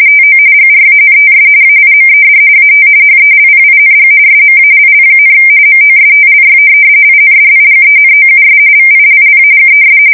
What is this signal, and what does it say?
RTTY, about 45.45 baud. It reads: TO GRANTS AIMED AT HELPING SMALL, MOSTLY INDIGENOUS VILLAGES
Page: 170 Shift 2125/2295